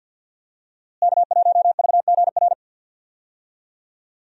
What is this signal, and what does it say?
Morse code, heard as a keyed tone, about 49 wpm, 700 Hz. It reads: K1VGR
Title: USA Call Signs - 5 Characters - Repeated 2x - 50wpm